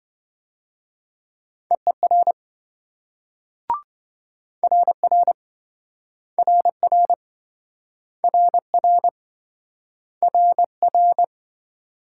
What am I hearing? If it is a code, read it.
Morse code, heard as a keyed tone, about 24 wpm, 700 Hz. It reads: EER RR RR RR RR
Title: Speed-Racing Multi-letter Phonemes - 20wpm